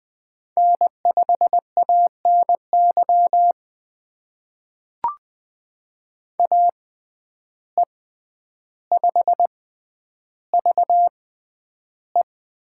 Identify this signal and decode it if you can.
Morse code, heard as a keyed tone, about 20 wpm, spaced as for 10 wpm, 700 Hz. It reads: N5ANY AE5VE
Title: USA Call Signs - 5 Characters - 6x Character Spacing - 20wpm